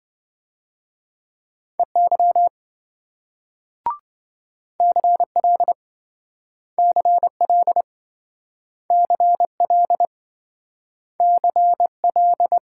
Morse code, heard as a keyed tone, about 26 wpm, 700 Hz. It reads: EY CL CL CL CL